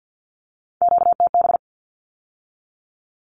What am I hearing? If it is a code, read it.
Morse code, heard as a keyed tone, about 50 wpm, 700 Hz. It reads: QTB
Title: Q-Codes with Spoken Meaning - 50wpm